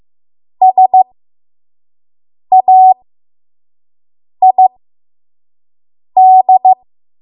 Morse code, heard as a keyed tone, about 15 wpm, spaced as for 5 wpm, 750 Hz. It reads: SAID
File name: Morse / 05WPM